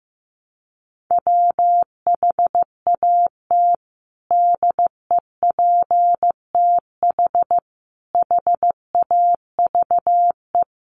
Morse code, heard as a keyed tone, about 15 wpm, 700 Hz. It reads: WHAT DEPTH HAVE
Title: Sentences from Top 1200 Words 15wpm